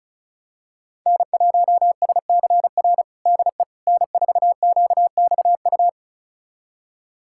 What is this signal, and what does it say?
Morse code, heard as a keyed tone, about 36 wpm, 700 Hz. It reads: N1SCR DE N4QXU
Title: USA Callsign DE Callsign - 5 Characters - 35wpm